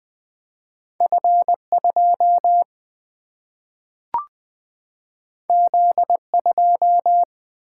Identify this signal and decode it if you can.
Morse code, heard as a keyed tone, about 20 wpm, 700 Hz. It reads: F 2 Z 2
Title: Single-Letter Single-Number 20wpm